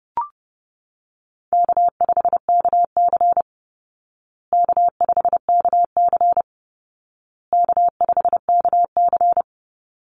Morse code, heard as a keyed tone, about 30 wpm, 700 Hz. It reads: K5KC K5KC K5KC
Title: Easy Call Signs Repeated 3x - 30wpm